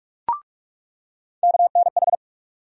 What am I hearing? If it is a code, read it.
Morse code, heard as a keyed tone, about 43 wpm, 700 Hz. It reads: KNH
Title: USA Call Sign Postfixes - 3 Letters 45wpm